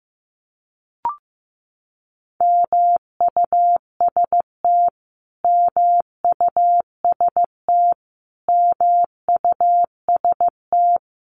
Morse code, heard as a keyed tone, about 15 wpm, 700 Hz. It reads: MUST MUST MUST
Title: Top 500 Words Repeated 15wpm